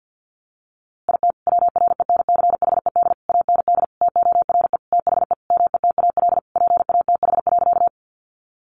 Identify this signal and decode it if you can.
Morse code, heard as a keyed tone, about 50 wpm, 700 Hz. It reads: IT WRENCHED AND TORE THE METAL PATH.
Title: A Sound Of Thunder 50wpm